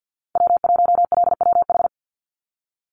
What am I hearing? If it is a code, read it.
Morse code, heard as a keyed tone, about 50 wpm, 700 Hz. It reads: W1LWH
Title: Call Signs Repeated 3x - 50wpm